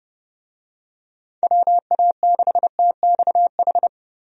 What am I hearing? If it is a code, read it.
Morse code, heard as a keyed tone, about 30 wpm, 700 Hz. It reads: WA6TXH
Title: USA Call Signs - 6 Characters - Repeated 2x - 30wpm